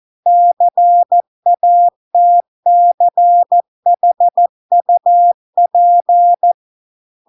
Morse code, heard as a keyed tone, about 14 wpm, 700 Hz. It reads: CATCHUP